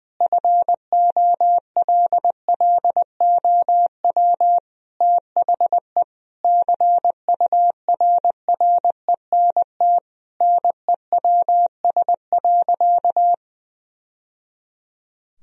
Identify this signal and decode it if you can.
Morse code, heard as a keyed tone, about 21 wpm, 700 Hz. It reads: FOLLOW THE CURRENT NEWS.
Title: Sentences from Top 1000 Words - Review Part 4 - 20wpm